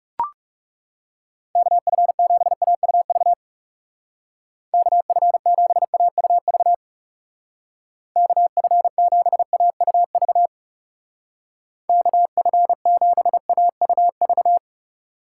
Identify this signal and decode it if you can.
Morse code, heard as a keyed tone, about 36 wpm, 700 Hz. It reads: KF7AUV KF7AUV KF7AUV KF7AUV
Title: Speed Racing - USA 2x3 Call Signs 30wpm